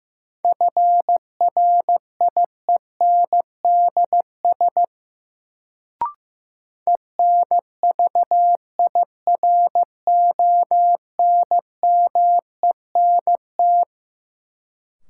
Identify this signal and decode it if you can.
Morse code, heard as a keyed tone, about 15 wpm, 700 Hz. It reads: FRIENDS ENVIRONMENT
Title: Sentences from Top 1000 Words - Review Part 5 - New Word Drill - 15wpm